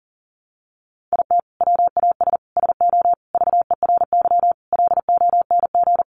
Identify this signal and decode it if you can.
Morse code, heard as a keyed tone, about 40 wpm, 700 Hz. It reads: IT WAS SO VERY LONG
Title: Sentences from Top 500 Words 40wpm